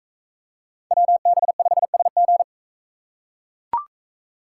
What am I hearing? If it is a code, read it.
Morse code, heard as a keyed tone, about 44 wpm, 700 Hz. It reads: WB5SG